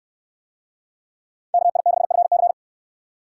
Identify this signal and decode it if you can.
Morse code, heard as a keyed tone, about 69 wpm, 700 Hz. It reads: KI6FC